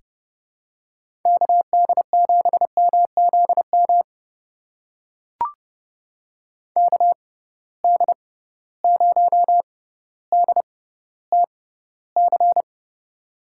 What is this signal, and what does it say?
Morse code, heard as a keyed tone, about 30 wpm, spaced as for 15 wpm, 700 Hz. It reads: KD7MZM KD0DTC
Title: USA Call Signs - 6 Characters - 6x Character Spacing - 30wpm